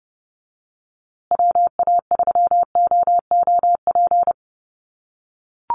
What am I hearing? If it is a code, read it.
Morse code, heard as a keyed tone, about 30 wpm, 700 Hz. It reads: WA3OOP
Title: USA Call Signs Repeated 3x 30wpm